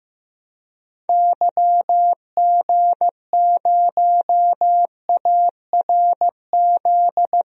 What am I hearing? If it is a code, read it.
Morse code, heard as a keyed tone, about 15 wpm, 700 Hz. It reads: YG0ARZ
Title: DXCC-100 Call Signs - 6 Characters - 15wpm